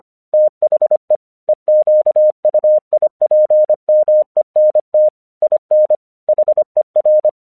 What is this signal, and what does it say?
Morse code, heard as a keyed tone, about 24 wpm, 610 Hz. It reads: THE EQUIPMENT IN HER
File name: 1 English 25wpm / AQC12